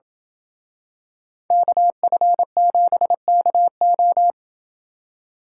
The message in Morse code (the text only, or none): KF7KO